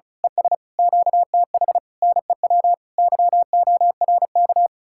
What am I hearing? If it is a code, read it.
Morse code, heard as a keyed tone, about 32 wpm, 700 Hz. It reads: ES QTH NEW YORK
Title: Speed-Racing Common QSO Phrases Encore - 35wpm